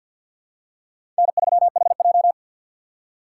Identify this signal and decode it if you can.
Morse code, heard as a keyed tone, about 52 wpm, 700 Hz. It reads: N3HJ
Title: USA Call Signs - 4 Characters - 2x Character Spacing - 50wpm